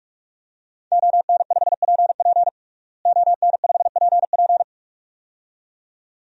Morse code, heard as a keyed tone, about 41 wpm, 700 Hz. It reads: ON5PP ON5PP
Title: DXCC-100 Call Signs - 5 Characters - Repeated 2x - 45wpm